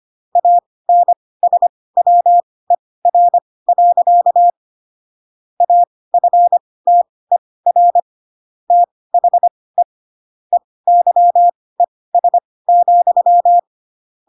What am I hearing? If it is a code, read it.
Morse code, heard as a keyed tone, about 25 wpm, spaced as for 18 wpm, 700 Hz. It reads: ANSWER. AFTER THE EYES,
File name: Pinocchio ch3 12 25 ik1hge